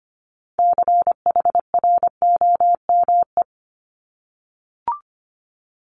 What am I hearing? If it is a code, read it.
Morse code, heard as a keyed tone, about 25 wpm, 700 Hz. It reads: CHROME
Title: Multi-letter Phonemes 25wpm